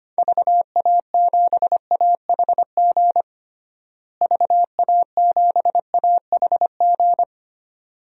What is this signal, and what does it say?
Morse code, heard as a keyed tone, about 25 wpm, 700 Hz. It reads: VA7AHG VA7AHG
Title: DXCC-100 Call Signs - 6 Characters - Repeated 2x - 25wpm